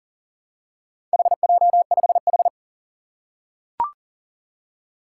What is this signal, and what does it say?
Morse code, heard as a keyed tone, about 38 wpm, 700 Hz. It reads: HJ5H
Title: Four Challenging Letter-Number Combinations 40wpm